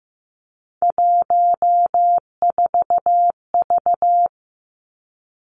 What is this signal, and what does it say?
Morse code, heard as a keyed tone, about 15 wpm, 700 Hz. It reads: 14V
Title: Three Challenging Letter-Number Combinations 15wpm